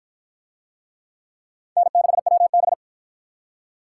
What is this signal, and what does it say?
Morse code, heard as a keyed tone, about 53 wpm, 700 Hz. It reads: N6WB